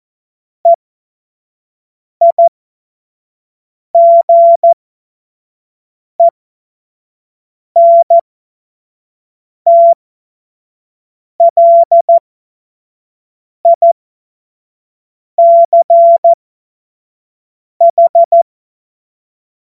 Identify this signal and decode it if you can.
Morse code, heard as a keyed tone, about 14 wpm, spaced as for 5 wpm, 680 Hz. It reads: EIGENTLICH
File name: Morsekurs / Witze / Witz 25. Juli 2023 25 F70